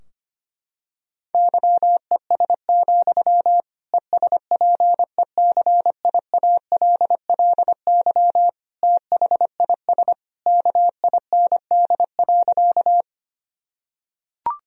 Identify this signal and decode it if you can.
Morse code, heard as a keyed tone, about 25 wpm, 700 Hz. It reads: YES, ESPECIALLY THIS KIND.
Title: Sentences from Top 1000 Words - Review Part 4 - 25wpm